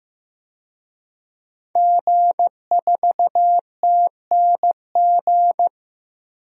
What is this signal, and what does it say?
Morse code, heard as a keyed tone, about 15 wpm, 700 Hz. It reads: G4TNG
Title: DXCC-100 Call Signs - 5 Characters - Repeated 2x - 15wpm